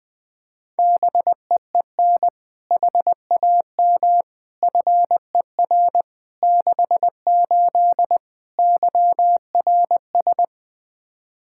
Morse code, heard as a keyed tone, about 21 wpm, 700 Hz. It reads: BEEN HAM FER 68 YRS
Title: Practice QSO - Exchange 3 - More Info - 20wpm